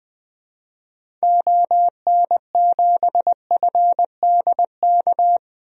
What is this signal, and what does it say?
Morse code, heard as a keyed tone, about 20 wpm, 700 Hz. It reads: ON7FDK
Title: DXCC-100 Call Signs - 6 Characters - Repeated 3x - 20wpm